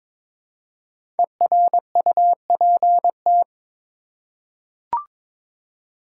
Morse code, heard as a keyed tone, about 22 wpm, 700 Hz. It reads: ERUPT
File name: 5 Letter Words - First 3 Letters Given - 22wpm